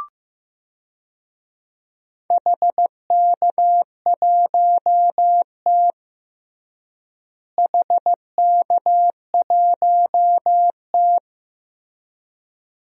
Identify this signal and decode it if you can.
Morse code, heard as a keyed tone, about 15 wpm, 700 Hz. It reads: HK1T HK1T
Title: DXCC-100 Call Signs - 4 Characters - Repeated 2x - 15wpm